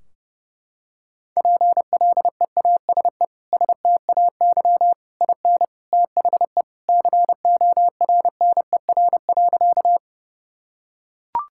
Morse code, heard as a keyed tone, about 30 wpm, 700 Hz. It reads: PLEASE STAY IN THE CORNER.
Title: Sentences from Top 1600 Words 30wpm